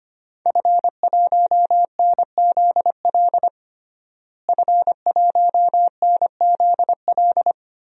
Morse code, heard as a keyed tone, about 25 wpm, 700 Hz. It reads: F1NZL F1NZL
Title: DXCC-100 Call Signs - 5 Characters - Repeated 3x - 25wpm